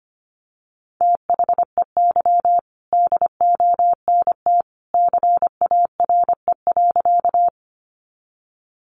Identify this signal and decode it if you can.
Morse code, heard as a keyed tone, about 25 wpm, 700 Hz. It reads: THEY DONT CARE.
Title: Sentences from Top 300 Words 25wpm